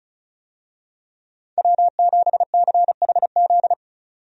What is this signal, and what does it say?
Morse code, heard as a keyed tone, about 36 wpm, 700 Hz. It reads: W7CHZ
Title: USA Call Signs - 5 Characters - 35wpm